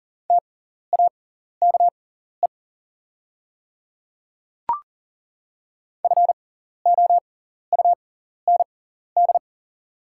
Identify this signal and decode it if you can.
Morse code, heard as a keyed tone, about 40 wpm, spaced as for 14 wpm, 690 Hz. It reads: TAKE FOUND
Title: 6x Character Spacing - Top 300 Words - 40wpm